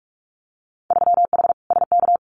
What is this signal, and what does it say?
Morse code, heard as a keyed tone, about 46 wpm, 700 Hz. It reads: 3H SK
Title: Winter Field Day Exchanges 45wpm